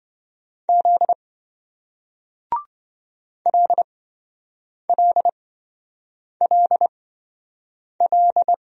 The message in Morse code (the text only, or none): Z LLLL